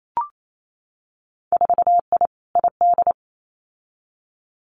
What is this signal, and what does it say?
Morse code, heard as a keyed tone, about 28 wpm, 700 Hz. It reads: 4I ID
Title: Winter Field Day Exchanges 28wpm